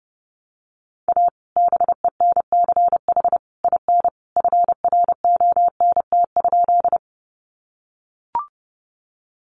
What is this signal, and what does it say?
Morse code, heard as a keyed tone, about 30 wpm, 700 Hz. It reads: A BENCH IN FRONT?
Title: Sentences from Top 1400 Words 30wpm